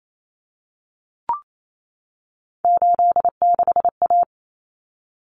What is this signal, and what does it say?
Morse code, heard as a keyed tone, about 28 wpm, 700 Hz. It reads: 86A
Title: Common Cut Numbers - 3-Digit 28wpm